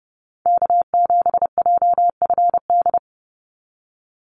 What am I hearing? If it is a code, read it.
Morse code, heard as a keyed tone, about 30 wpm, 700 Hz. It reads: K7JFD
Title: NA SOTA Activator Callsigns 30wpm